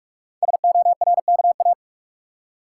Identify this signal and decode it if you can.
Morse code, heard as a keyed tone, about 44 wpm, 690 Hz. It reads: SORKA
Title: Code Groups - 5 Letters - 45wpm